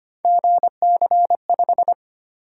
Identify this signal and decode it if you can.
Morse code, heard as a keyed tone, about 25 wpm, 700 Hz. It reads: GC5
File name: Three Letter-Number Combinations - 25wpm